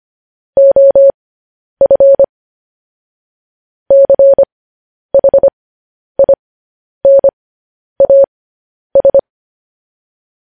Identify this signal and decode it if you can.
Morse code, heard as a keyed tone, about 25 wpm, spaced as for 10 wpm, 550 Hz. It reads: OF CHINAS